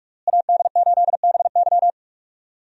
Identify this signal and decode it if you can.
Morse code, heard as a keyed tone, about 46 wpm, 700 Hz. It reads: AD8BY
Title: USA Call Signs - 5 Characters - Repeated 2x - 45wpm